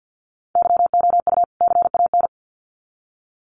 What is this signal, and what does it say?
Morse code, heard as a keyed tone, about 50 wpm, 700 Hz. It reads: YOU CAN
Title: Sets of 2 Words 50wpm